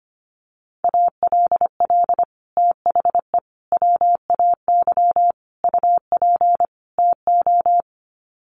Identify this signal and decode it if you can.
Morse code, heard as a keyed tone, about 25 wpm, 700 Hz. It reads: ALL THE WAY UP TO
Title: Sets of 5 Words Encore 25wpm